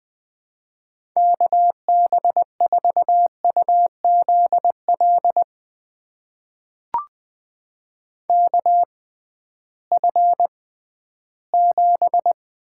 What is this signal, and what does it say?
Morse code, heard as a keyed tone, about 21 wpm, spaced as for 13 wpm, 700 Hz. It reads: KB4UZL KF7
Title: USA Call Signs - 6 Characters - 6x Character Spacing - 20wpm